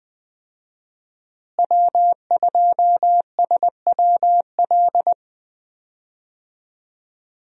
Morse code, heard as a keyed tone, about 20 wpm, 700 Hz. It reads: W2SWL
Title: USA Call Signs - 5 Characters - Repeated 2x - 20wpm